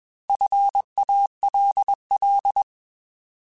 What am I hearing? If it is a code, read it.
Morse code, heard as a keyed tone, about 21 wpm, 800 Hz. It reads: FALL
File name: Alphabetical / Alphabetical mp3s / Fs @20